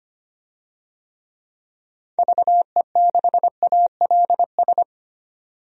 Morse code, heard as a keyed tone, about 25 wpm, 700 Hz. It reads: VE6ALS
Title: DXCC-100 Call Signs - 6 Characters - 25wpm